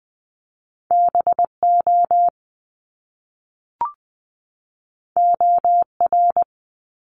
Morse code, heard as a keyed tone, about 20 wpm, 700 Hz. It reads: BO OR
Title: Top Two Letter Combinations in Words 20wpm